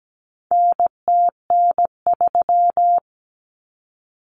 Morse code, heard as a keyed tone, about 17 wpm, 700 Hz. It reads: NTN3
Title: Common Cut Numbers - 4-Digit 17wpm